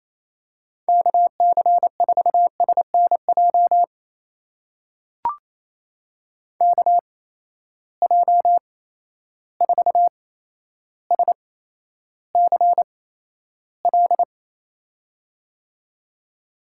Morse code, heard as a keyed tone, about 28 wpm, spaced as for 12 wpm, 700 Hz. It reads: KC4SNJ KJ4SCL
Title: USA Call Signs - 6 Characters - 8x Character Spacing - 28wpm